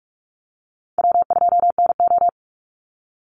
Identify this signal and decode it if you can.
Morse code, heard as a keyed tone, about 44 wpm, 700 Hz. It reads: W2NO
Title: Call Signs Repeated 3x - 45wpm